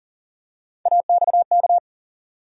Morse code, heard as a keyed tone, about 40 wpm, 700 Hz. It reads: AXK
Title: USA Call Sign Postfixes - 3 Letters 40wpm